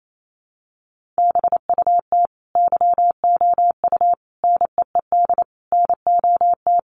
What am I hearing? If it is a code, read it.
Morse code, heard as a keyed tone, about 28 wpm, 700 Hz. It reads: BUT YOU NEED NOT
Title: Sentences from Top 1300 Words 28wpm